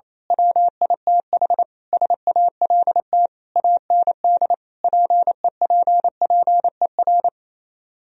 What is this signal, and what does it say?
Morse code, heard as a keyed tone, about 26 wpm, 700 Hz. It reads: WITH SALT AND PEPPER
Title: Speed-Racing - Sets of 5 Words 28wpm